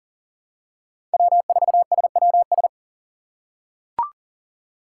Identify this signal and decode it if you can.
Morse code, heard as a keyed tone, about 40 wpm, 700 Hz. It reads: W4SWS
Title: USA Call Signs - 5 Characters - 3x Character Spacing - 40wpm